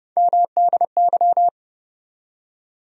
Morse code, heard as a keyed tone, about 30 wpm, 700 Hz. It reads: MDY
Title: USA Call Sign Postfixes - 3 Letters 30wpm